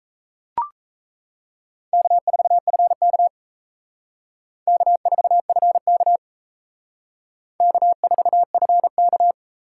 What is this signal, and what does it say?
Morse code, heard as a keyed tone, about 35 wpm, 700 Hz. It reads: K4FK K4FK K4FK
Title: Speed Racing - USA 1x2 Call Signs 28wpm